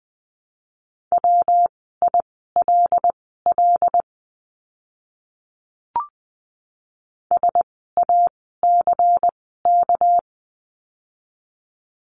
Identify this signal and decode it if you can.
Morse code, heard as a keyed tone, about 20 wpm, spaced as for 14 wpm, 700 Hz. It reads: WILL SACK
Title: 199 Lesson 32f - Letter k - Words 20wpm